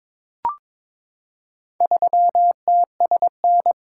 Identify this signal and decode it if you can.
Morse code, heard as a keyed tone, about 22 wpm, 700 Hz. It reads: 3TSN
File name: practice / 22wpm / Four Letter-Number Combinations - 22wpm